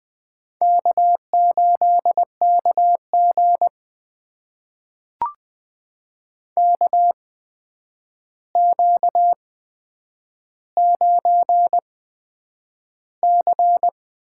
Morse code, heard as a keyed tone, about 20 wpm, spaced as for 9 wpm, 700 Hz. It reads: K8KG KQ9C
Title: USA Call Signs - 4 Characters - 8x Character Spacing - 20wpm